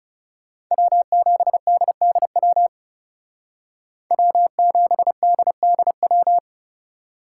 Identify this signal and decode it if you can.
Morse code, heard as a keyed tone, about 34 wpm, 700 Hz. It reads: W7DDW W7DDW
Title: Speed Racing - USA 1x3 Call Signs 30wpm